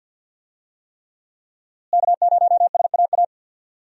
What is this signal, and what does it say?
Morse code, heard as a keyed tone, about 48 wpm, 700 Hz. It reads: K0SAA
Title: USA Call Signs - 5 Characters - Repeated 3x - 50wpm